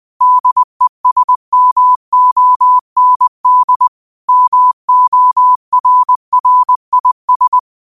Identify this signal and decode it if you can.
Morse code, heard as a keyed tone, about 20 wpm, 1000 Hz. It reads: DESMOND MORRIS
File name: 20wpm